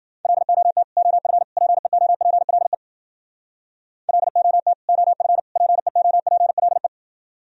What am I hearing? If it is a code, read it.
Morse code, heard as a keyed tone, about 62 wpm, 700 Hz. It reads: LOT OF PEOPLE LOT OF PEOPLE